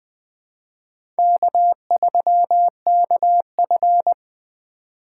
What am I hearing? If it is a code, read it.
Morse code, heard as a keyed tone, about 20 wpm, 700 Hz. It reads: K3KF
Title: USA Call Signs - 4 Characters - 4x Character Spacing - 20wpm